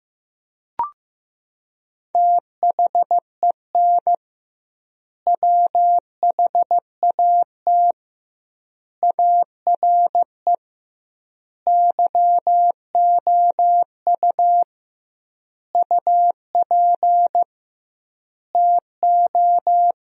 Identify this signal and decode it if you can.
Morse code, heard as a keyed tone, about 15 wpm, 700 Hz. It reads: THEN WHAT ARE YOU UP TO
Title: 2x Word-Spacing - Sentences from Top 100 Words - 15wpm